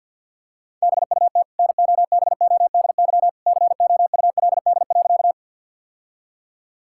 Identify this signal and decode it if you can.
Morse code, heard as a keyed tone, about 50 wpm, 700 Hz. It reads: BUT NOBODY COULD.
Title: Sentences from Top 1600 Words 50wpm